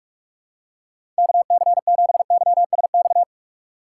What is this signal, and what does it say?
Morse code, heard as a keyed tone, about 47 wpm, 700 Hz. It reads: KC7YSX
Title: USA Call Signs - 2x Character Spacing - 45wpm